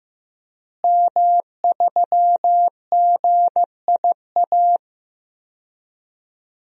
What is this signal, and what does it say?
Morse code, heard as a keyed tone, about 15 wpm, 700 Hz. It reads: M3GIA
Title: DXCC-100 Call Signs - 5 Characters - Repeated 2x - 15wpm